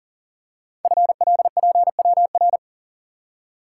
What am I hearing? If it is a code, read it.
Morse code, heard as a keyed tone, about 43 wpm, 700 Hz. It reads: FLPWR
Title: Code Groups - 5 Letters - 40wpm